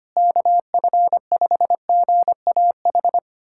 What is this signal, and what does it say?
Morse code, heard as a keyed tone, about 25 wpm, 700 Hz. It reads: KF5GAH
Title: USA Callsign DE Callsign 25wpm